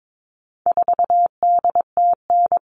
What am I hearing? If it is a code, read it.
Morse code, heard as a keyed tone, about 22 wpm, 700 Hz. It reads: 4DTN
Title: All Cut Numbers - 4-Digit 22wpm